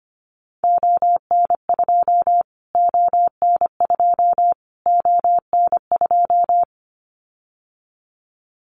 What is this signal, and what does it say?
Morse code, heard as a keyed tone, about 25 wpm, 700 Hz. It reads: ON2 ON2 ON2
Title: Callsign Prefixes for Top 25 DXCC - Repeated 3x - 25wpm